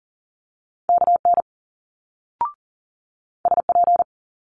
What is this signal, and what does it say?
Morse code, heard as a keyed tone, about 42 wpm, 700 Hz. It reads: KN SP
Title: Multi-letter Phonemes Quick Fire 40wpm